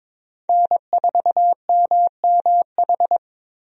Morse code, heard as a keyed tone, about 22 wpm, 700 Hz. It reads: N4MMH
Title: USA Call Signs - 5 Characters - Repeated 3x - 22wpm